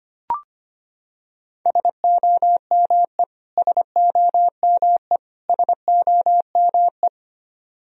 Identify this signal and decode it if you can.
Morse code, heard as a keyed tone, about 25 wpm, 700 Hz. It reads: SOME SOME SOME
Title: Top 100 Words Repeated 25wpm